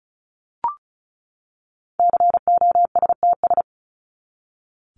Morse code, heard as a keyed tone, about 34 wpm, 700 Hz. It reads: COSTS
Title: ES:S Suffix - Top 100 Words with Base Words 35wpm